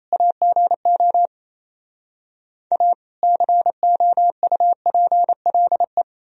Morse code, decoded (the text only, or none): AGO A COUPLE